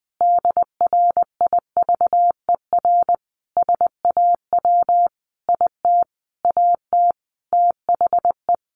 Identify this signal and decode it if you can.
Morse code, heard as a keyed tone, about 20 wpm, 700 Hz. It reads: DRIVER SAW IT AT THE
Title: Sentences from Top 1000 Words 20wpm